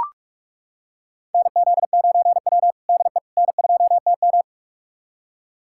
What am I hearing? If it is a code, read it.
Morse code, heard as a keyed tone, about 41 wpm, 700 Hz. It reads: NZ9W DE N2TM
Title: USA Callsign DE Callsign - 4 Characters - 45wpm